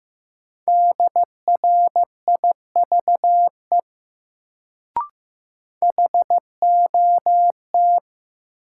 Morse code, heard as a keyed tone, about 15 wpm, 700 Hz. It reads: DRIVE HOT
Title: Sentences from Top 500 Words - New Word Drill - 15wpm